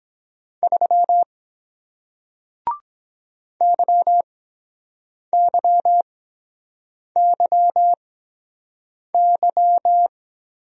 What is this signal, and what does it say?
Morse code, heard as a keyed tone, about 23 wpm, spaced as for 7 wpm, 700 Hz. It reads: 3 YYYY